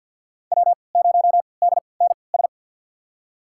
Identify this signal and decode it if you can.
Morse code, heard as a keyed tone, about 46 wpm, spaced as for 27 wpm, 700 Hz. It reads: W0DNS
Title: USA Call Signs - 5 Characters - 3x Character Spacing - 50wpm